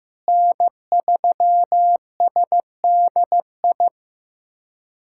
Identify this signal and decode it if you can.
Morse code, heard as a keyed tone, about 15 wpm, 700 Hz. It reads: N3SDI
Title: USA Callsign DE Callsign 15wpm